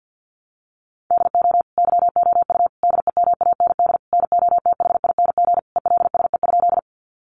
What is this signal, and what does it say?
Morse code, heard as a keyed tone, about 50 wpm, 700 Hz. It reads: DO YOU DEMAND NOTHING ELSE?
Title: Sentences from Top 800 Words 50wpm